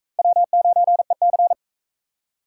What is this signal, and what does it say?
Morse code, heard as a keyed tone, about 38 wpm, 700 Hz. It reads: W9EC